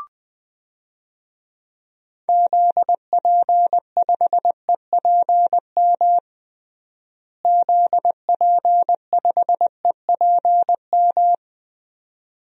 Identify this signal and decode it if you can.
Morse code, heard as a keyed tone, about 20 wpm, 700 Hz. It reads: ZP5EPM ZP5EPM
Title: DXCC-100 Call Signs - 6 Characters - Repeated 2x - 20wpm